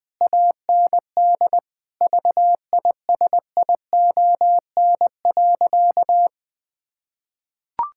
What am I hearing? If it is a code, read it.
Morse code, heard as a keyed tone, about 20 wpm, 700 Hz. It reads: AND VISION.
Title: Difficult Sentences with Hints - 20wpm